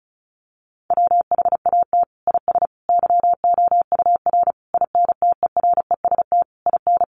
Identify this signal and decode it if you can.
Morse code, heard as a keyed tone, about 34 wpm, 700 Hz. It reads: WHAT IS YOUR INTEREST IN
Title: Sentences from Top 1300 Words 35wpm